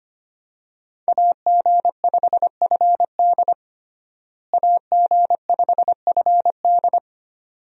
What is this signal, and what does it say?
Morse code, heard as a keyed tone, about 25 wpm, 700 Hz. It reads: AG5FD AG5FD
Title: USA Call Signs - 5 Characters - Repeated 2x - 25wpm